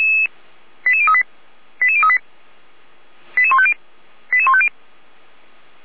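Five-tone selective call (ZVEI), heard as quick runs of five tones, tones of about 70 ms. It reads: 88038 88038 88150 88150
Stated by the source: ZVEI five-tone selective call 89700 received on SDR.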